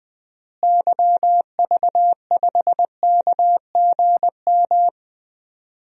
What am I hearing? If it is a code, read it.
Morse code, heard as a keyed tone, about 21 wpm, 700 Hz. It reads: YV5KGM
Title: DXCC-100 Call Signs - 6 Characters - Repeated 3x - 20wpm